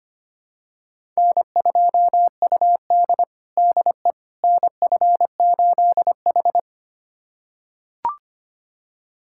Morse code, heard as a keyed tone, about 25 wpm, 700 Hz. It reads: N2UD DE NF8H
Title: USA Callsign DE Callsign - 4 Characters - 25wpm